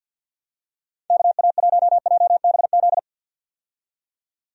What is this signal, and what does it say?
Morse code, heard as a keyed tone, about 51 wpm, 700 Hz. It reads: KA1JBZ
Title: USA Call Signs - 6 Characters - 50wpm